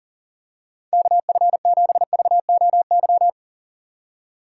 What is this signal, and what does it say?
Morse code, heard as a keyed tone, about 40 wpm, 700 Hz. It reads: KF7VOY
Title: USA Call Signs - 6 Characters - 2x Character Spacing - 40wpm